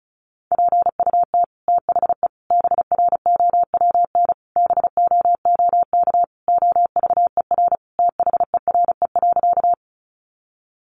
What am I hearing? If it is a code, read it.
Morse code, heard as a keyed tone, about 34 wpm, 700 Hz. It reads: PUT THE BROWN BOOK OVER THERE.
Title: Sentences from Top 1200 Words 35wpm